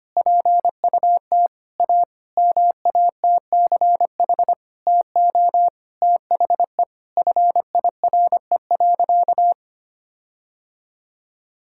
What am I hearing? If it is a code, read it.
Morse code, heard as a keyed tone, about 25 wpm, 700 Hz. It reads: PUT A MATCH TO THE FIRE.
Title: Sentences from Top 1600 Words 25wpm